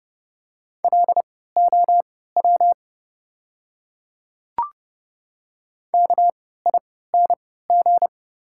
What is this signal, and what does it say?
Morse code, heard as a keyed tone, about 30 wpm, spaced as for 17 wpm, 700 Hz. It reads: LOW KING
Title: 3x Character Spacing - Top 500 Words - 30wpm